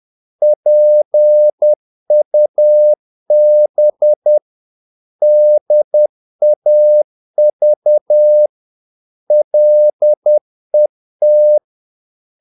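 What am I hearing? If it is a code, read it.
Morse code, heard as a keyed tone, about 10 wpm, 600 Hz. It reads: PUB DAV LET
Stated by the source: Morse practice of short Danish words at 10 wpm, from CW med Gnister.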